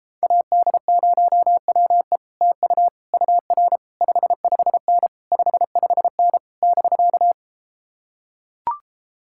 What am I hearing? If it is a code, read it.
Morse code, heard as a keyed tone, about 33 wpm, 700 Hz. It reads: AD0WE TU UR 55N 55N <BK>